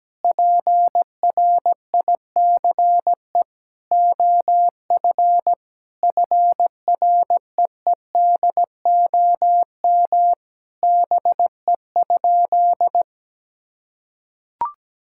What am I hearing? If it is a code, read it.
Morse code, heard as a keyed tone, about 17 wpm, 700 Hz. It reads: PRICE OF FREEDOM BE?
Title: Sentences from Top 1000 Words - Review Part 5 - 17wpm